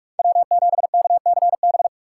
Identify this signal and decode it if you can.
Morse code, heard as a keyed tone, about 44 wpm, 700 Hz. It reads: W7KCB
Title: USA Call Signs - 5 Characters - 6x Character Spacing - 45wpm